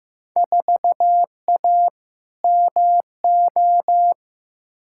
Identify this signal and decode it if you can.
Morse code, heard as a keyed tone, about 15 wpm, 700 Hz. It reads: 4A MO
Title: Field Day Exchanges - 15wpm